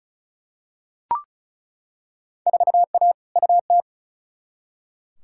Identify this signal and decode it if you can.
Morse code, heard as a keyed tone, about 36 wpm, 700 Hz. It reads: 4A UT
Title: Field Day Exchanges - 35wpm